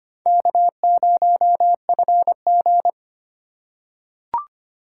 Morse code, heard as a keyed tone, about 25 wpm, 700 Hz. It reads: K0FG
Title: USA Call Signs - 4 Characters - 3x Character Spacing - 25wpm